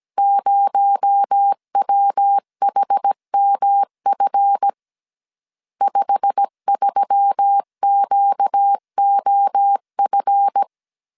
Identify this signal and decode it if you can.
Morse code, heard as a keyed tone, about 17 wpm, 800 Hz. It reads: 0WHMF 53QOF
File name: CW / DET / 1 9 1500C S C 16wpm 2909